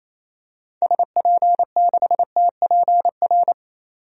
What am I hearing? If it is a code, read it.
Morse code, heard as a keyed tone, about 28 wpm, 700 Hz. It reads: SP6TPR
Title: DXCC-100 Call Signs - 6 Characters - 28wpm